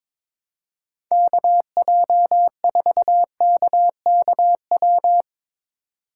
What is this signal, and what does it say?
Morse code, heard as a keyed tone, about 22 wpm, 700 Hz. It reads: KJ4KKW
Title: USA Call Signs - 6 Characters - 8x Character Spacing - 22wpm